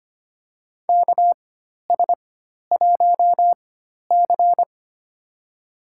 Morse code, heard as a keyed tone, about 25 wpm, spaced as for 12 wpm, 700 Hz. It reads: KS1C
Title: USA Call Signs - 4 Characters - 4x Character Spacing - 25wpm